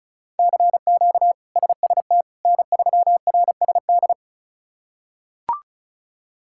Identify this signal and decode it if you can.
Morse code, heard as a keyed tone, about 36 wpm, 700 Hz. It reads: CQ SST N3RSD
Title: K1USN SST QSO Practice - 35wpm